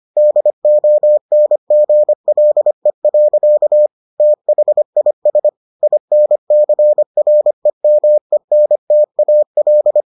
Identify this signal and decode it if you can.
Morse code, heard as a keyed tone, about 25 wpm, 600 Hz. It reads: DONGLE. THIS INCREMENTAL